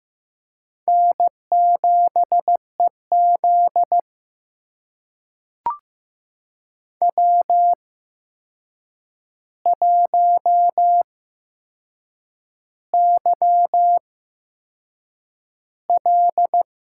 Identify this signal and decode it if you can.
Morse code, heard as a keyed tone, about 15 wpm, spaced as for 6 wpm, 700 Hz. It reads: N7EZ W1YL
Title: USA Call Signs - 4 Characters - 8x Character Spacing - 15wpm